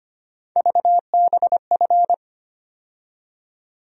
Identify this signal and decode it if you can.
Morse code, heard as a keyed tone, about 25 wpm, 700 Hz. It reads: VBF
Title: USA Call Sign Postfixes - 3 Letters 25wpm